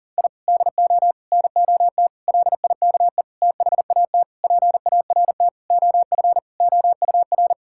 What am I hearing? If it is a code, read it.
Morse code, heard as a keyed tone, about 37 wpm, 690 Hz. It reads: I DO NOT LIKE THAT PART OF OUR
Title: Sentences from Top 1000 Words - Review Part 6 - 40wpm